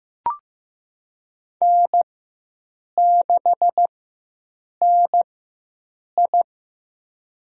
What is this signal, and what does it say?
Morse code, heard as a keyed tone, about 15 wpm, spaced as for 7 wpm, 700 Hz. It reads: N6NI
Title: USA Call Signs - 4x Character Spacing - 15wpm